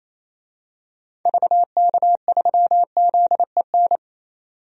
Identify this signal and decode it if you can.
Morse code, heard as a keyed tone, about 28 wpm, 700 Hz. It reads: VK3ZEN
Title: DXCC-100 Call Signs - 6 Characters - 28wpm